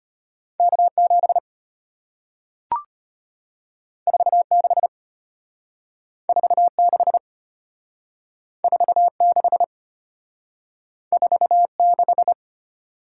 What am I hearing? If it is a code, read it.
Morse code, heard as a keyed tone, about 34 wpm, 700 Hz. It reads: K7 46 46 46 46